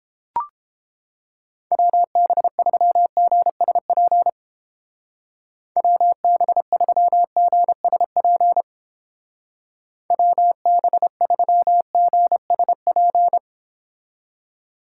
Morse code, heard as a keyed tone, about 30 wpm, 700 Hz. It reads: WB3GSP WB3GSP WB3GSP